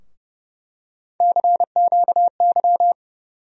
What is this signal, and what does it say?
Morse code, heard as a keyed tone, about 31 wpm, 700 Hz. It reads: CQY
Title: USA Call Sign Postfixes - 3 Letters 30wpm